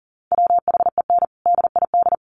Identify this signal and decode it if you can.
Morse code, heard as a keyed tone, about 40 wpm, 700 Hz. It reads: WHEN DID
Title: Sentences from Top 400 Words 40wpm